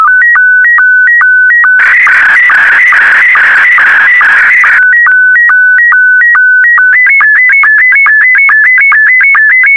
SSTV (slow-scan television), Scottie 1.